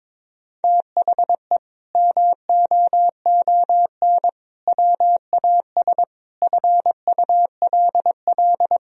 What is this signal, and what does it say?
Morse code, heard as a keyed tone, about 22 wpm, 700 Hz. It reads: THE MOON WAS FULL
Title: Sentences from Top 1000 Words - Review Part 3 - 22wpm